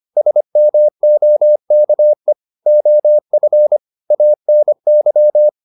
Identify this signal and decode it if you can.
Morse code, heard as a keyed tone, about 25 wpm, 600 Hz. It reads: SMOKE OF ANY